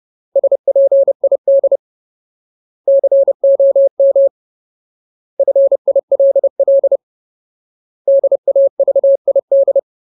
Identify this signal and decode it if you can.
Morse code, heard as a keyed tone, about 30 wpm, 550 Hz. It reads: SPID COM FILL DAVID
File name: Ultimate Word List / 30WPM Ultimate